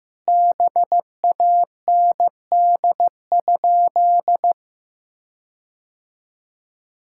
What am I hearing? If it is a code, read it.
Morse code, heard as a keyed tone, about 16 wpm, 700 Hz. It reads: BAND?
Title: Sentences from Top 1600 Words 15wpm